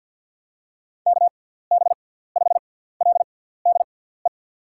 Morse code, heard as a keyed tone, about 47 wpm, spaced as for 17 wpm, 700 Hz. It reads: KB5LDE
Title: USA Call Signs - 6 Characters - 6x Character Spacing - 50wpm